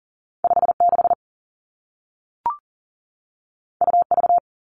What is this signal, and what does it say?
Morse code, heard as a keyed tone, about 38 wpm, 700 Hz. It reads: 5 6 U V
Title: Two Challenging Letter-Number Combinations 40wpm